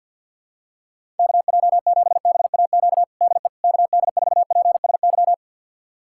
Morse code, heard as a keyed tone, about 51 wpm, 700 Hz. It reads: KJ7BAQ DE KD4PSY
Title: USA Callsign DE Callsign 50wpm